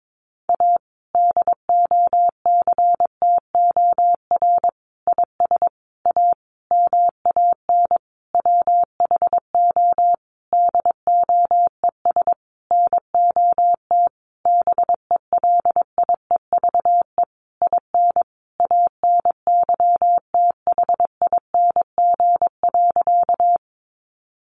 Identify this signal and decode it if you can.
Morse code, heard as a keyed tone, about 22 wpm, 700 Hz. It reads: A DOCTOR IS A MAN WHO DOES NOT BELIEVE IN ANYTHING.
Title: Sentences from Top 1500 Words 22wpm